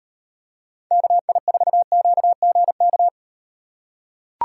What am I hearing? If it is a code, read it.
Morse code, heard as a keyed tone, about 38 wpm, 700 Hz. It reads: KI4QGK